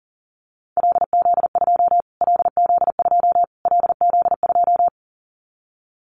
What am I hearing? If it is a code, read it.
Morse code, heard as a keyed tone, about 37 wpm, 700 Hz. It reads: LZ2 LZ2 LZ2
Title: Callsign Prefixes for Top 25 DXCC - Repeated 3x - 40wpm